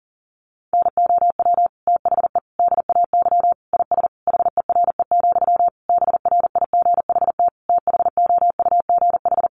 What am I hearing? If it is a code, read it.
Morse code, heard as a keyed tone, about 40 wpm, 700 Hz. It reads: NOW THE DAY IS HERE, BRIGHT THOUGH
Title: Sentences from Top 1000 Words 40wpm